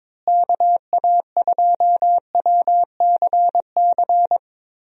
Morse code, heard as a keyed tone, about 22 wpm, 700 Hz. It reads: KA2WCC
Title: USA Call Signs - 6 Characters - Repeated 3x - 22wpm